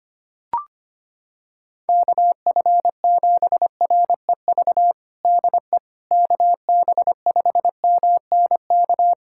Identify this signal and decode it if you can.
Morse code, heard as a keyed tone, about 25 wpm, 700 Hz. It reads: KF7REV DE KB5MNK
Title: USA Callsign DE Callsign - 6 Characters - 25wpm